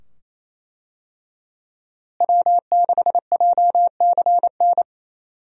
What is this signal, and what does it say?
Morse code, heard as a keyed tone, about 27 wpm, 700 Hz. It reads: W6JCN
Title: USA Call Signs - 5 Characters - Repeated 2x - 28wpm